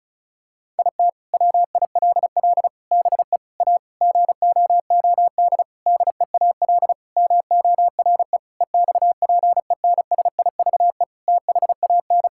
Morse code, heard as a keyed tone, about 36 wpm, 700 Hz. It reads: IT WILL BE A GOOD DEAL MORE EXPENSIVE THAN
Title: Sentences from Top 1000 Words - Review Part 6 - 35wpm